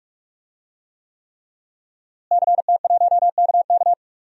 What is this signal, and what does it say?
Morse code, heard as a keyed tone, about 46 wpm, 700 Hz. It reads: CT1KK
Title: DXCC-100 Call Signs - 5 Characters - Repeated 3x - 45wpm